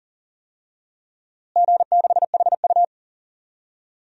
Morse code, heard as a keyed tone, about 39 wpm, 700 Hz. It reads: G6HU
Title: DXCC-100 Call Signs - 4 Characters - Repeated 2x - 40wpm